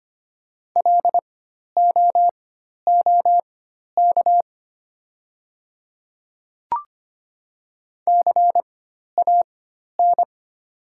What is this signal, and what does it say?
Morse code, heard as a keyed tone, about 25 wpm, spaced as for 12 wpm, 700 Hz. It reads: LOOK CAN
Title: 4x Character Spacing - Top 200 Words - 25wpm